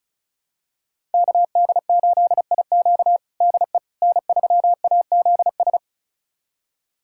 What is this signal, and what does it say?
Morse code, heard as a keyed tone, about 36 wpm, 700 Hz. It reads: KD8IQ DE N3AZS
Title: USA Callsign DE Callsign - 5 Characters - 35wpm